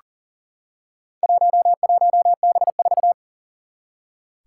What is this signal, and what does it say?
Morse code, heard as a keyed tone, about 40 wpm, 700 Hz. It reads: 11B4
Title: Four Challenging Letter-Number Combinations 40wpm